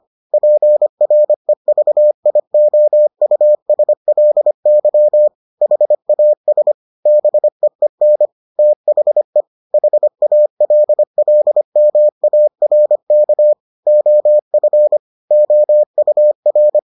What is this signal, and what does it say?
Morse code, heard as a keyed tone, about 25 wpm, 600 Hz. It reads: PREVIOUSLY HAS BEEN THE HALLMARK OF OUR